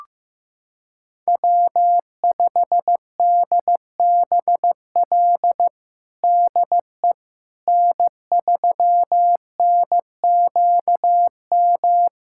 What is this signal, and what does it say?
Morse code, heard as a keyed tone, about 15 wpm, 700 Hz. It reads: W5DBL DE N3NQM
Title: USA Callsign DE Callsign - 5 Characters - 15wpm